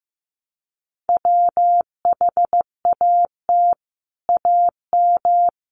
Morse code, heard as a keyed tone, about 14 wpm, 700 Hz. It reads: WHAT AM
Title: Sets of 5 Words Encore2 15wpm